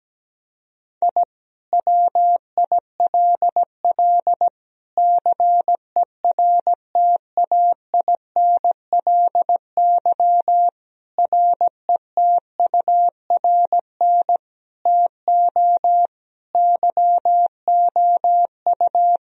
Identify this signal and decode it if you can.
Morse code, heard as a keyed tone, about 17 wpm, 700 Hz. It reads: I WILL CERTAINLY RETURN TO YOU
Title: Sentences from Top 1000 Words - Review Part 4 - 17wpm